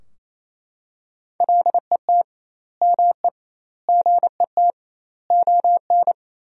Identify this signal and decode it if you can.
Morse code, heard as a keyed tone, about 28 wpm, 700 Hz. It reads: LET ME GET ON
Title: 2x Word-Spacing - Sentences from Top 300 Words - 28wpm